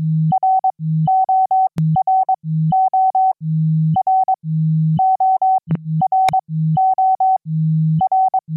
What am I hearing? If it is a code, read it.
Morse code, heard as a keyed tone, about 22 wpm, spaced as for 15 wpm, 760 Hz. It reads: RORO R ORO R
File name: QSO-PA3FXB-DJ5AR